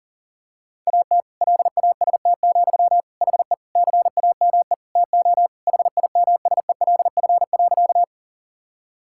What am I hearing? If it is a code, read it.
Morse code, heard as a keyed tone, about 40 wpm, 700 Hz. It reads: AT LAST, HE CAME TO HIMSELF.
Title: Sentences from Top 1600 Words 40wpm